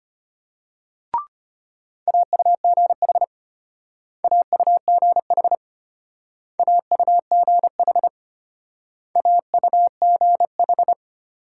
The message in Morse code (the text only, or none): AUGH AUGH AUGH AUGH